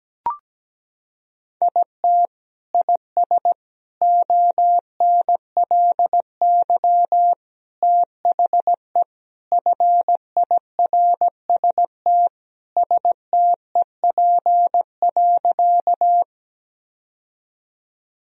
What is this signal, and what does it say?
Morse code, heard as a keyed tone, about 17 wpm, 700 Hz. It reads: IT IS ONLY THE FIRST STEP.
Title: Sentences from Top 1000 Words - Review Part 2 - 17wpm